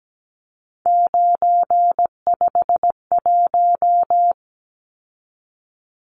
Morse code, heard as a keyed tone, about 17 wpm, 700 Hz. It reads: 951
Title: Numbers - 3-Digit 17wpm